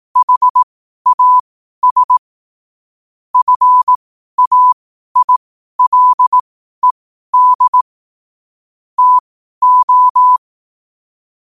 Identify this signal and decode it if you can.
Morse code, heard as a keyed tone, about 18 wpm, spaced as for 12 wpm, 1000 Hz. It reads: HAS FAILED TO
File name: qotdfiles / 12wpm